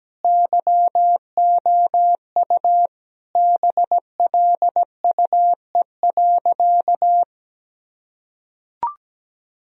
Morse code, heard as a keyed tone, about 17 wpm, 700 Hz. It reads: YOU BLUE.
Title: Sentences from Top 1000 Words - Review Part 3 - 17wpm